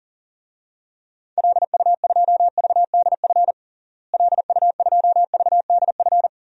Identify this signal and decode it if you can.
Morse code, heard as a keyed tone, about 37 wpm, 700 Hz. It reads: LU2VDF LU2VDF
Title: DXCC-100 Call Signs - 6 Characters - Repeated 3x - 40wpm